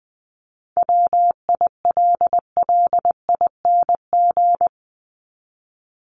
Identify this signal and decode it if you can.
Morse code, heard as a keyed tone, about 20 wpm, 700 Hz. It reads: WILLING
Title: ING Suffix - Top 100 Words with Base Words 20wpm